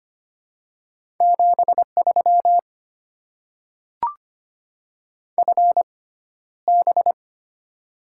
Morse code, heard as a keyed tone, about 25 wpm, spaced as for 13 wpm, 700 Hz. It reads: 73 FB
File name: Top 20 QSO Elements - 6x Character Spacing - 25wpm